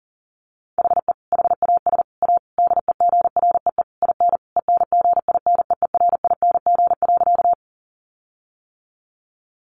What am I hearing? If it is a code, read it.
Morse code, heard as a keyed tone, about 40 wpm, 700 Hz. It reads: HE HAS A DEGREE IN ENGINEERING.
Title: Sentences from Top 1500 Words 40wpm